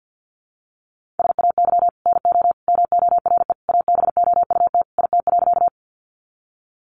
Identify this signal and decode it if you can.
Morse code, heard as a keyed tone, about 50 wpm, 700 Hz. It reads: SAY NO MORE ABOUT IT.
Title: Sentences from Top 200 Words 50wpm